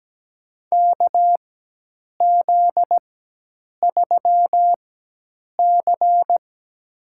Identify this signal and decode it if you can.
Morse code, heard as a keyed tone, about 17 wpm, spaced as for 8 wpm, 700 Hz. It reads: KZ3C
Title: USA Call Signs - 4 Characters - 4x Character Spacing - 17wpm